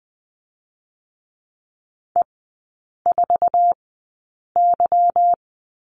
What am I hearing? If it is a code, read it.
Morse code, heard as a keyed tone, about 20 wpm, spaced as for 8 wpm, 700 Hz. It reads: E4Y
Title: 145 Lesson 24e - Letter y - Three Characters 20wpm